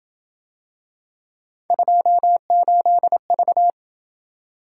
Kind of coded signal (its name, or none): Morse code